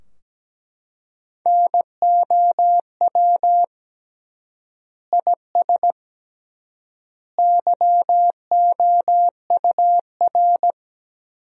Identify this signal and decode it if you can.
Morse code, heard as a keyed tone, about 17 wpm, 700 Hz. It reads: NOW IS YOUR
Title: 3x Word-Spacing - Sentences from Top 100 Words - 17wpm